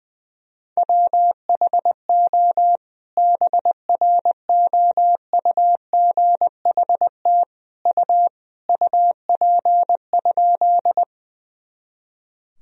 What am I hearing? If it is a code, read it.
Morse code, heard as a keyed tone, about 20 wpm, 700 Hz. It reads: WHO BROUGHT U UP?
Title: Sentences from Top 500 Words and Abbreviations 20wpm